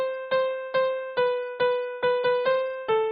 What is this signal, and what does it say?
There is no code.